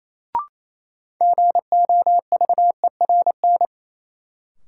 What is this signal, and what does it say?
Morse code, heard as a keyed tone, about 28 wpm, 700 Hz. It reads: GOVERN
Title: Sentences from Top 1000 Words - Review Part 2 - New Word Drill - 28wpm